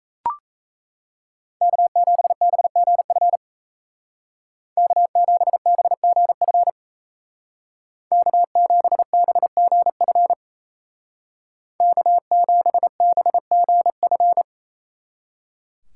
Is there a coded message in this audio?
Yes: Morse code from 1.6 to 14.4 s, heard as a keyed tone, about 36 wpm, 700 Hz. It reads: K7BGF K7BGF K7BGF K7BGF